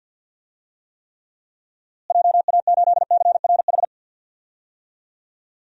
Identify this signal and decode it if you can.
Morse code, heard as a keyed tone, about 46 wpm, 700 Hz. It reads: JA8CRH
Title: DXCC-100 Call Signs - 6 Characters - Repeated 3x - 50wpm